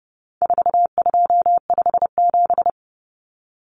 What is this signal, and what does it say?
Morse code, heard as a keyed tone, about 30 wpm, 700 Hz. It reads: 4257
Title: Numbers - 4-Digit 30wpm